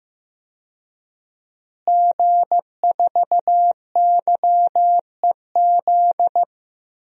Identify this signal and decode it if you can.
Morse code, heard as a keyed tone, about 15 wpm, 700 Hz. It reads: G4YEZ
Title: DXCC-100 Call Signs - 5 Characters - Repeated 3x - 15wpm